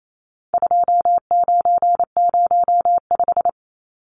Morse code, heard as a keyed tone, about 28 wpm, 700 Hz. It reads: 2905
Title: Numbers - 4-Digit 28wpm